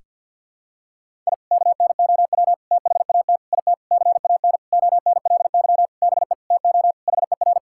Morse code, heard as a keyed tone, about 50 wpm, 700 Hz. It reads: I KNOW THAT IT CAN ONLY BE TO HER
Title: Sentences from Top 1000 Words - Review Part 6 - 50wpm